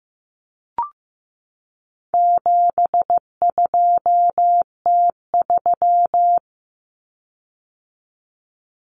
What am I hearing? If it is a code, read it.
Morse code, heard as a keyed tone, about 14 wpm, 700 Hz. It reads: 72T3
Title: Common Cut Numbers - 4-Digit 15wpm